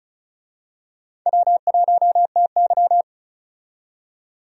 Morse code, heard as a keyed tone, about 36 wpm, 700 Hz. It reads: W1TY
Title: USA Call Signs - 4 Characters - 35wpm